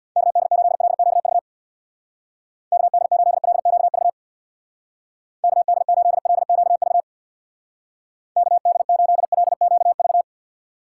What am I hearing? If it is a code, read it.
Morse code, heard as a keyed tone, about 64 wpm, 700 Hz. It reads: KD8LQV KD8LQV KD8LQV KD8LQV